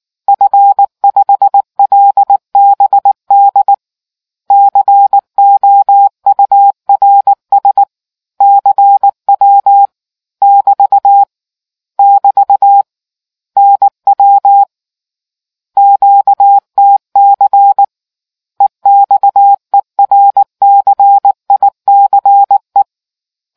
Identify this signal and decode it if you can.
Morse code, heard as a keyed tone, about 19 wpm, 800 Hz. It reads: F5LBD COURS CW <BT> <BT> NW QTC EXERCICE